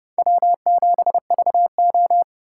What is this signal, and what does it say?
Morse code, heard as a keyed tone, about 28 wpm, 700 Hz. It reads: W7VO